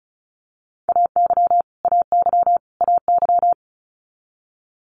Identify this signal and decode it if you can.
Morse code, heard as a keyed tone, about 36 wpm, 700 Hz. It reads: AY AY AY
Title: Multi-letter Phonemes Repeated 3x 35wpm